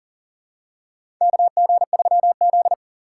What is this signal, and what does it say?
Morse code, heard as a keyed tone, about 40 wpm, 700 Hz. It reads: KG3Z
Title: USA Call Signs - 2x Character Spacing - 40wpm